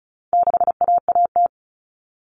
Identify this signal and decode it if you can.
Morse code, heard as a keyed tone, about 34 wpm, 700 Hz. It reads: 6AAT
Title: Common Cut Numbers - 4-Digit 35wpm